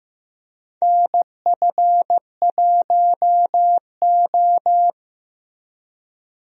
Morse code, heard as a keyed tone, about 15 wpm, 700 Hz. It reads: NF1O
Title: USA Call Signs - 4 Characters - Repeated 2x - 15wpm